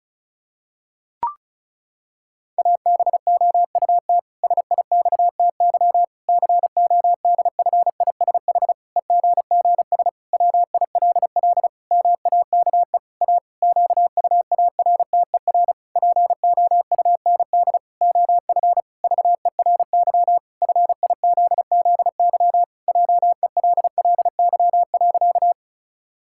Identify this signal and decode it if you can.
Morse code, heard as a keyed tone, about 36 wpm, 700 Hz. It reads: ABOUT SIXTY CODFISH EGGS WILL MAKE A QUARTER POUND OF VERY FIZZY JELLY.
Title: Pangram Challenge v2 - 35wpm